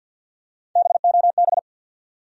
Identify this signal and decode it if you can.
Morse code, heard as a keyed tone, about 51 wpm, 700 Hz. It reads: BOB
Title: USA Common Names - Max 4 Letters - 50wpm